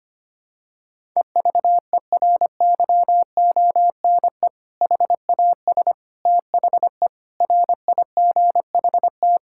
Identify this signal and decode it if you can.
Morse code, heard as a keyed tone, about 25 wpm, 700 Hz. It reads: EVERYONE HAS THE RIGHT
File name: practice / 25wpm / Sentences with 3-Word Phrases - 25wpm